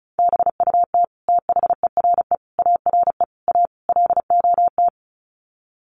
Morse code, heard as a keyed tone, about 34 wpm, 700 Hz. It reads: BUT THERE ARE A LOT
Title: Sets of 5 Words Encore2 35wpm